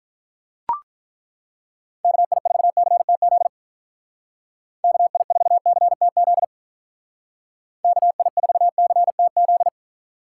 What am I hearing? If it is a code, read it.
Morse code, heard as a keyed tone, about 42 wpm, 700 Hz. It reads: KI4CTZ KI4CTZ KI4CTZ
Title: Speed Racing - USA 2x3 Call Signs 35wpm